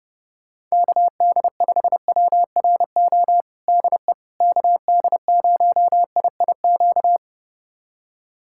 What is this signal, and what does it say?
Morse code, heard as a keyed tone, about 30 wpm, 700 Hz. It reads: KD5WRO DE KD0IIQ
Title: USA Callsign DE Callsign - 6 Characters - 30wpm